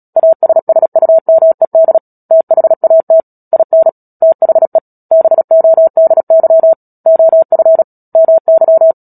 Morse code, heard as a keyed tone, about 37 wpm, 650 Hz. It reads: ASSUMED THAT IN THE BODY OF MY
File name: Book Two / 10-Chapter 10 - Epilogue - 36 WPM